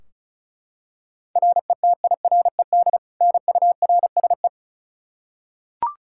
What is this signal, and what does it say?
Morse code, heard as a keyed tone, about 36 wpm, 700 Hz. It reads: RETIRED NURSE
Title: Practice QSO - Exchange 3 - More Info - 35wpm